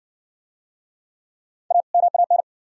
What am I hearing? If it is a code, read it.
Morse code, heard as a keyed tone, about 58 wpm, 700 Hz. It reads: A MAN